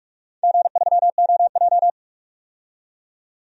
Morse code, heard as a keyed tone, about 43 wpm, 700 Hz. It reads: G3OJ
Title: DXCC-100 Call Signs - 4 Characters - 45wpm